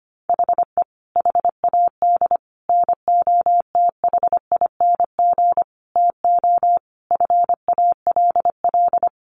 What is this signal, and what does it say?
Morse code, heard as a keyed tone, about 25 wpm, 700 Hz. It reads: HE HAD NOTHING TO FALL
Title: Sentences from Top 1100 Words 25wpm